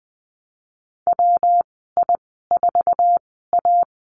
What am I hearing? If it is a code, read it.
Morse code, heard as a keyed tone, about 19 wpm, spaced as for 14 wpm, 700 Hz. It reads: WI4A
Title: 128 Lesson 21g - Number 6 - Call Signs 20wpm